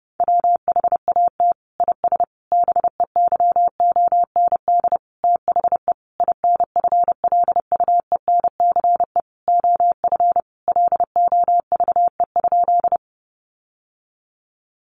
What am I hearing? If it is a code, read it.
Morse code, heard as a keyed tone, about 30 wpm, 700 Hz. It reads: WHAT IS BEYOND THE INFLUENCE OF LOVE?
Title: Sentences from Top 1100 Words 30wpm